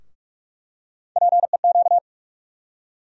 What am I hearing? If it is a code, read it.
Morse code, heard as a keyed tone, about 46 wpm, 700 Hz. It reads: PEQ
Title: USA Call Sign Postfixes - 3 Letters 45wpm